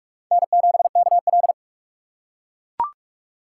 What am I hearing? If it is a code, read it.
Morse code, heard as a keyed tone, about 40 wpm, 700 Hz. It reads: N7KL